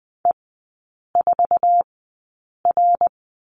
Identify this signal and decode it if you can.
Morse code, heard as a keyed tone, about 20 wpm, spaced as for 8 wpm, 700 Hz. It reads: E4R
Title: 053 Lesson 10e - Letter r - Three Characters 20wpm